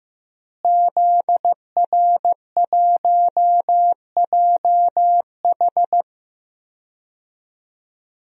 Morse code, heard as a keyed tone, about 16 wpm, 700 Hz. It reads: ZR1JH
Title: DXCC-100 Call Signs - 5 Characters - Repeated 2x - 15wpm